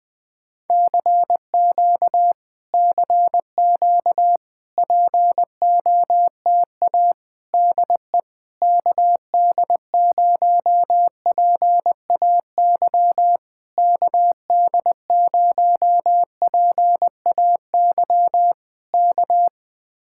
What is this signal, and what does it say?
Morse code, heard as a keyed tone, about 20 wpm, 700 Hz. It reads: CQ CQ POTA DE KD0PAY KD0PAY K